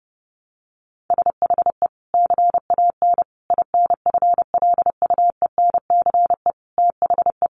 Morse code, heard as a keyed tone, about 30 wpm, 700 Hz. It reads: SHE CAN INFLUENCE THE
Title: Sentences from Top 1100 Words 30wpm